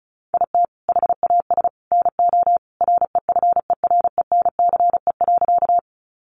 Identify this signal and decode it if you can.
Morse code, heard as a keyed tone, about 34 wpm, 700 Hz. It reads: IT HAS NO REFERENCE.
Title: Sentences from Top 1200 Words 35wpm